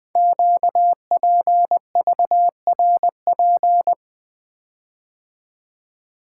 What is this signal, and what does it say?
Morse code, heard as a keyed tone, about 20 wpm, 700 Hz. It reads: QPVRP
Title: Code Groups - 5 Letters - 20wpm